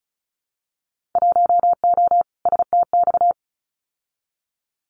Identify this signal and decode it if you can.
Morse code, heard as a keyed tone, about 34 wpm, 700 Hz. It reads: 1O STX
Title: Winter Field Day Exchanges 35wpm